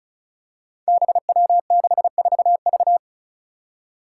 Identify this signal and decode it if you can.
Morse code, heard as a keyed tone, about 36 wpm, 700 Hz. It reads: BW64V
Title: Five Challenging Letter-Number Combinations 35wpm